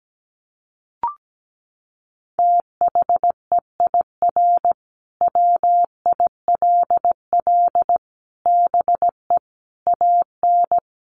Morse code, heard as a keyed tone, about 17 wpm, 700 Hz. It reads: THEIR WILL BE AN
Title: Sentences from Top 1500 Words 17wpm